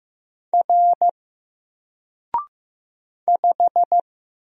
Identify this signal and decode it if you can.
Morse code, heard as a keyed tone, about 15 wpm, 700 Hz. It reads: R 5
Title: Single Letter-Number Alt 15wpm